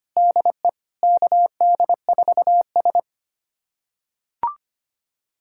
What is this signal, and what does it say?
Morse code, heard as a keyed tone, about 25 wpm, 700 Hz. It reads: DE KD4S
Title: USA Callsign DE Callsign - 4 Characters - 25wpm